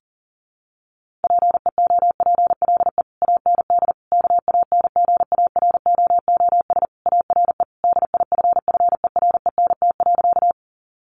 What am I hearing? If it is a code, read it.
Morse code, heard as a keyed tone, about 37 wpm, 700 Hz. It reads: PEOPLE AND KANGAROOS ARE DIFFERENT.
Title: Easy Phrases 40wpm